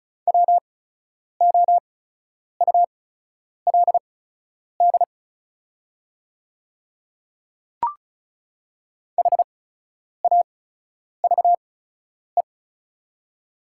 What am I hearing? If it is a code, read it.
Morse code, heard as a keyed tone, about 34 wpm, spaced as for 10 wpm, 690 Hz. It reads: WOULD HAVE
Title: 8x Character Spacing - Top 100 Words - 35wpm